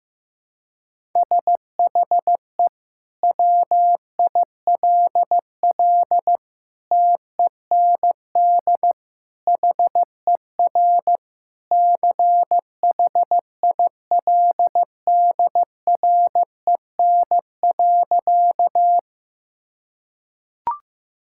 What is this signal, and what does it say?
Morse code, heard as a keyed tone, about 15 wpm, 700 Hz. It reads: SHE WILL TEND HER CHILDREN.
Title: Sentences from Top 1000 Words - Review Part 5 - 15wpm